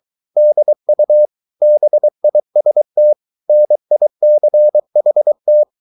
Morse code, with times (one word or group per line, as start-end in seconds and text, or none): 0.4-1.3 DU
1.6-3.1 BIST
3.5-5.6 NICHT